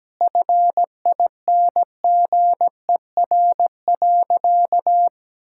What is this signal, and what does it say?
Morse code, heard as a keyed tone, about 17 wpm, 700 Hz. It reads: FINGER.
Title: Sentences from Top 1600 Words 17wpm